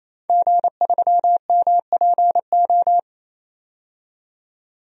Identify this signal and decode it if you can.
Morse code, heard as a keyed tone, about 27 wpm, 700 Hz. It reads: G3MPO
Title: DXCC-100 Call Signs - 5 Characters - 28wpm